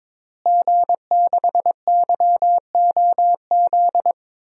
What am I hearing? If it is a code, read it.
Morse code, heard as a keyed tone, about 22 wpm, 700 Hz. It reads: G6YOZ
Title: DXCC-100 Call Signs - 5 Characters - Repeated 2x - 22wpm